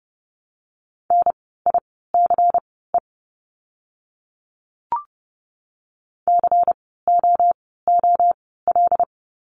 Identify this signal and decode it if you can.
Morse code, heard as a keyed tone, about 31 wpm, spaced as for 17 wpm, 700 Hz. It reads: NICE COOL
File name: Lesson 16f - Letter c - Words 30wpm